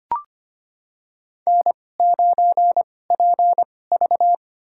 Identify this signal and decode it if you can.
Morse code, heard as a keyed tone, about 25 wpm, spaced as for 18 wpm, 700 Hz. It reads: N9PV
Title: USA Call Signs - 4 Characters - 2x Character Spacing - 25wpm